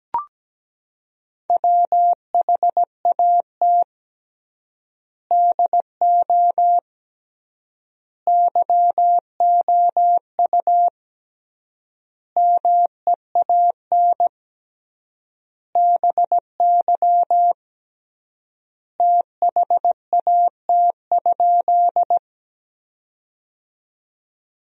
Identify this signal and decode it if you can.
Morse code, heard as a keyed tone, about 17 wpm, 700 Hz. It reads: WHAT DO YOU MEAN BY THAT?
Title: 3x Word-Spacing - Sentences from Top 200 Words - 17wpm